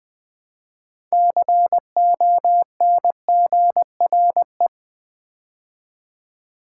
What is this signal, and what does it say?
Morse code, heard as a keyed tone, about 20 wpm, 700 Hz. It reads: CONGRE
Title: Build A Word - 6 Letters At A Time - 12-Letter Words - 20wpm